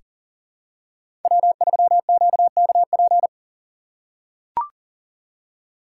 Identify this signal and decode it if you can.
Morse code, heard as a keyed tone, about 39 wpm, 700 Hz. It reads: W3QKP
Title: USA Call Signs - 5 Characters - 4x Character Spacing - 40wpm